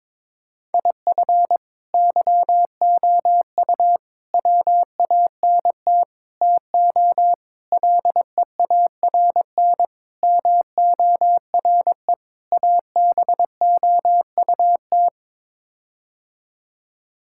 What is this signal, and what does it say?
Morse code, heard as a keyed tone, about 22 wpm, 700 Hz. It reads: IF YOU WANT TO LEARN MORE ABOUT
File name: practice / 22wpm / Sets of 7 Words - 22wpm